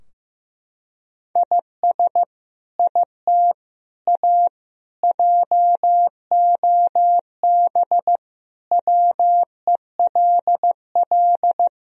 Morse code, heard as a keyed tone, about 15 wpm, 700 Hz. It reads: IS IT A JOB WELL
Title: Sentences from Top 1000 Words - Review Part 3 - 15wpm